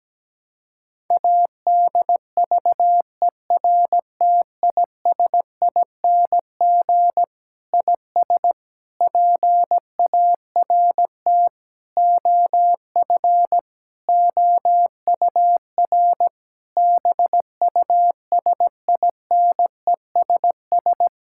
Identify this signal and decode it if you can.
Morse code, heard as a keyed tone, about 17 wpm, 700 Hz. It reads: ADVERTISING IS PART OF OUR BUSINESS
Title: Sentences from Top 1000 Words - Review Part 6 - 17wpm